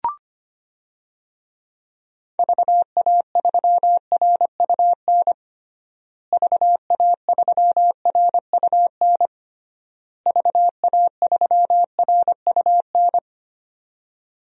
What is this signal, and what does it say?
Morse code, heard as a keyed tone, about 25 wpm, 700 Hz. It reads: VA3RUN VA3RUN VA3RUN
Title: DXCC-100 Call Signs - 6 Characters - Repeated 3x - 25wpm